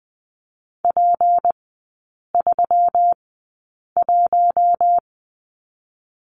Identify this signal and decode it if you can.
Morse code, heard as a keyed tone, about 20 wpm, spaced as for 8 wpm, 700 Hz. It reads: P31
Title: 191 Lesson 31e - Letter v - Three Characters 20wpm